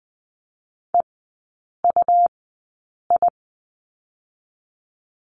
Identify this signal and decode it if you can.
Morse code, heard as a keyed tone, about 20 wpm, spaced as for 8 wpm, 700 Hz. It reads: EUI
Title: 100 Lesson 17e - Letter u - Three Characters 20wpm